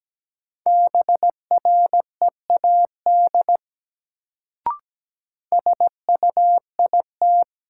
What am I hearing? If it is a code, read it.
Morse code, heard as a keyed tone, about 17 wpm, 700 Hz. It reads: BREAD SUIT
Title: Sentences from Top 1600 Words - New Word Drill Alt - 17wpm